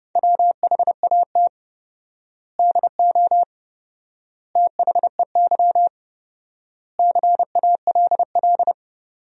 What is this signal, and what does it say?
Morse code, heard as a keyed tone, about 32 wpm, 700 Hz. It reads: WHAT DO THEY CALL
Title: 4x Word-Spacing - Sentences from Top 100 Words - 30wpm